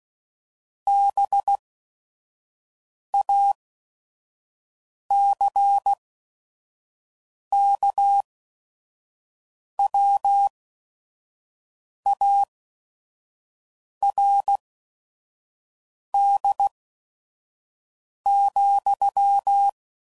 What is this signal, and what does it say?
Morse code, heard as a keyed tone, about 16 wpm, spaced as for 5 wpm, 800 Hz. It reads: BACKWARD,